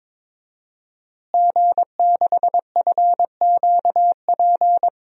Morse code, heard as a keyed tone, about 22 wpm, 700 Hz. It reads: G6FQP
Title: DXCC-100 Call Signs - 5 Characters - Repeated 3x - 22wpm